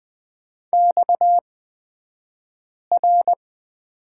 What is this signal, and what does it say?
Morse code, heard as a keyed tone, about 20 wpm, spaced as for 5 wpm, 700 Hz. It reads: XR
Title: 228 Lesson 37d - Letter x - Two Characters 20wpm